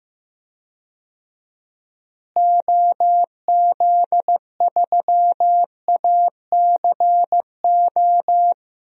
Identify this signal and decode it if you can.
Morse code, heard as a keyed tone, about 16 wpm, 700 Hz. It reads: OZ3ACO
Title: DXCC-100 Call Signs - 6 Characters - Repeated 3x - 15wpm